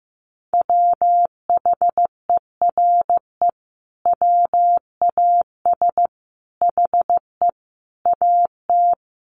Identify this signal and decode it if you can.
Morse code, heard as a keyed tone, about 15 wpm, 700 Hz. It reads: WHERE WAS HE AT
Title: Sentences from Top 1000 Words 15wpm